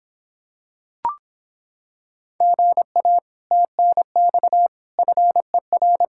Morse code, heard as a keyed tone, about 26 wpm, 700 Hz. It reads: GA TNX FER